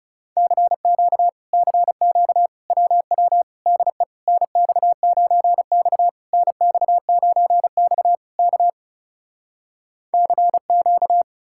Morse code, heard as a keyed tone, about 40 wpm, 700 Hz. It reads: CQ CQ WW DE NX9X NX9X K CQ